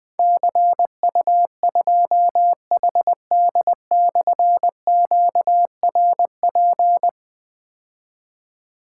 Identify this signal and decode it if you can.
Morse code, heard as a keyed tone, about 20 wpm, 700 Hz. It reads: CU2HD/QRP
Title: Difficult Call Signs 20wpm